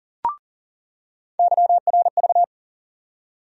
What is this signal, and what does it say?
Morse code, heard as a keyed tone, about 40 wpm, 700 Hz. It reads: YRV
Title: USA Call Sign Postfixes - 3 Letters 40wpm